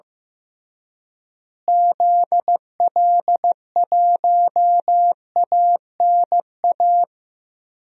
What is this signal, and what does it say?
Morse code, heard as a keyed tone, about 15 wpm, 700 Hz. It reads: ZL1ANA
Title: DXCC-100 Call Signs - 6 Characters - Repeated 3x - 15wpm